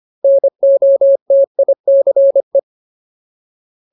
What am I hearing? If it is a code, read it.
Morse code, heard as a keyed tone, about 25 wpm, 550 Hz. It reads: NOTICE
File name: Ultimate Word List / 25WPM Ultimate